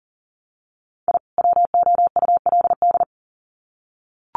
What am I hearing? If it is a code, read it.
Morse code, heard as a keyed tone, about 40 wpm, 700 Hz. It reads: I WOULD
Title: Sets of 2 Words 40wpm